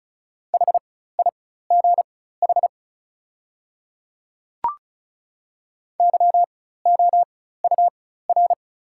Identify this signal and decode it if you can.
Morse code, heard as a keyed tone, about 35 wpm, spaced as for 16 wpm, 700 Hz. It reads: HIGH YOUR
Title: 4x Character Spacing - Top 300 Words - 35wpm